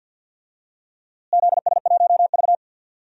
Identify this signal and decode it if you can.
Morse code, heard as a keyed tone, about 46 wpm, 700 Hz. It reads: ZS1V
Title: DXCC-100 Call Signs - 4 Characters - Repeated 2x - 50wpm